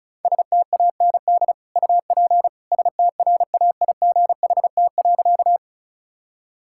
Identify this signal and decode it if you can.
Morse code, heard as a keyed tone, about 36 wpm, 700 Hz. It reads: STAND UP STRAIGHT.
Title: Sentences from Top 1000 Words - Review Part 6 - 35wpm